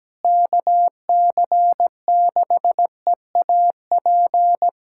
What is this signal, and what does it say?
Morse code, heard as a keyed tone, about 17 wpm, 700 Hz. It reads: KC6EAP
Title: USA Callsign DE Callsign 17wpm